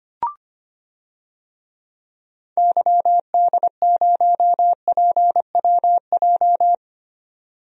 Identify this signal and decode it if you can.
Morse code, heard as a keyed tone, about 25 wpm, 700 Hz. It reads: YD0PWJ
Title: DXCC-100 Call Signs - 6 Characters - 25wpm